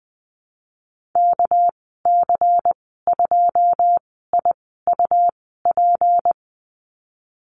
Morse code, heard as a keyed tone, about 20 wpm, spaced as for 15 wpm, 700 Hz. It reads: KC2IUP
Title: 231 Lesson 37g - Letter x - Call Signs 20wpm